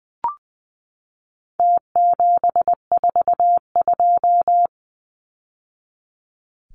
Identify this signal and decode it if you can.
Morse code, heard as a keyed tone, about 20 wpm, 700 Hz. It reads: T742
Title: Numbers - Contest Style - 4-Digit 20wpm